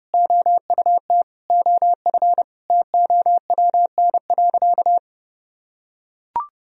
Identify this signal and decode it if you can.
Morse code, heard as a keyed tone, about 30 wpm, 700 Hz. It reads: OUT OF TOWN.
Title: Sentences from Top 1000 Words - Review Part 4 - 30wpm